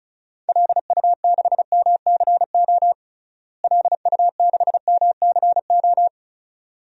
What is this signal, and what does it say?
Morse code, heard as a keyed tone, about 36 wpm, 700 Hz. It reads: LU6MCO LU6MCO
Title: DXCC-100 Call Signs - 6 Characters - Repeated 2x - 35wpm